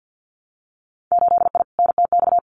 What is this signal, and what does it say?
Morse code, heard as a keyed tone, about 50 wpm, 700 Hz. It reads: 8I NTX
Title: Winter Field Day Exchanges Encore 50wpm